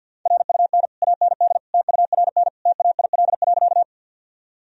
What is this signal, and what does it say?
Morse code, heard as a keyed tone, about 50 wpm, 700 Hz. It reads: RUN AND TURN TAIL.
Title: Sentences from Top 1000 Words - Review Part 3 - 50wpm